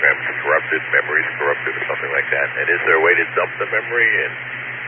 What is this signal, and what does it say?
Emergency Alert System (SAME) tones, heard as harsh buzzing bursts, with speech. It is heard from 0.6 to 4.1 s.